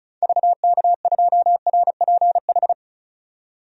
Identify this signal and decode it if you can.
Morse code, heard as a keyed tone, about 34 wpm, 700 Hz. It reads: VK2RPH
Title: DXCC-100 Call Signs - 6 Characters - 35wpm